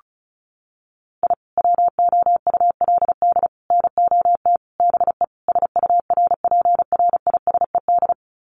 Morse code, heard as a keyed tone, about 34 wpm, 700 Hz. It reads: I WOULD NOT BE SURPRISED
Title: Sets of 5 Words Encore2 35wpm